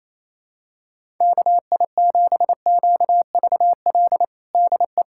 Morse code, heard as a keyed tone, about 28 wpm, 700 Hz. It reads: KI7QVL DE
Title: USA Callsign DE Callsign 28wpm